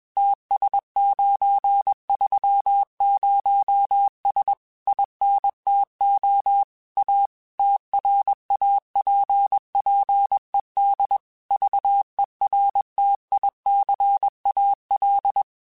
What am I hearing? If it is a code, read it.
Morse code, heard as a keyed tone, about 21 wpm, 800 Hz. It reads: TS930S INTO A TRAPPED VERTICAL
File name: QSOs / QSOs-mp3s / kc7haa@20